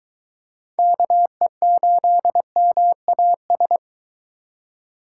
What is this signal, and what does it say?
Morse code, heard as a keyed tone, about 22 wpm, 700 Hz. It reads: KE8MAS